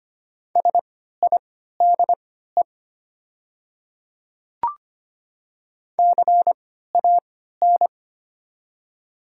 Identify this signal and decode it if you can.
Morse code, heard as a keyed tone, about 25 wpm, spaced as for 14 wpm, 700 Hz. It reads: SIDE CAN
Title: 3x Character Spacing - Top 100 Words - 25wpm